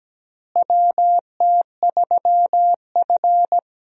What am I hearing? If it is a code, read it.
Morse code, heard as a keyed tone, about 17 wpm, 700 Hz. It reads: WT3F
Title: USA Call Signs - 4 Characters - Repeated 3x - 17wpm